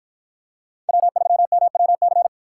Morse code, heard as a keyed tone, about 48 wpm, 700 Hz. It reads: W3MWC